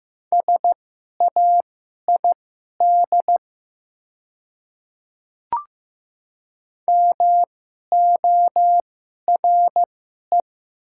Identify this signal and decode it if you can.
Morse code, heard as a keyed tone, about 15 wpm, spaced as for 11 wpm, 700 Hz. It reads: SAID MORE
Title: 2x Character Spacing - Top 500 Words - 15wpm